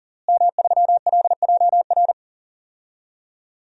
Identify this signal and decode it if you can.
Morse code, heard as a keyed tone, about 39 wpm, 700 Hz. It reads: M3LJR
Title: DXCC-100 Call Signs - 5 Characters - Repeated 3x - 40wpm